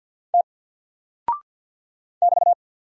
Morse code, heard as a keyed tone, about 50 wpm, 700 Hz. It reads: T <BT>
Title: Single Character (Letter:Number:Punctuation:Prosign) 50wpm